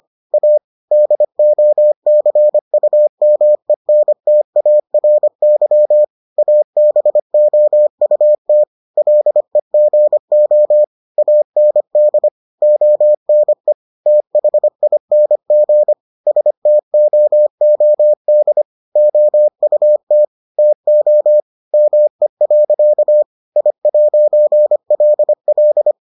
Morse code, heard as a keyed tone, about 25 wpm, 600 Hz. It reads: A DOCUMENTARY ABOUT LEGO AND ONE THING STOOD OUT TO ME. I'LL